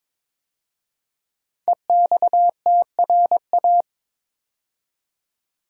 Morse code, heard as a keyed tone, about 22 wpm, 700 Hz. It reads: EXTRA
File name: 5 Letter Words - First 3 Letters Given - 22wpm